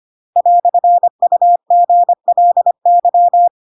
Morse code, heard as a keyed tone, about 25 wpm, 700 Hz. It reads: "UGLY
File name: Pinocchio / Pinocchio ch2 25 25 ik1hge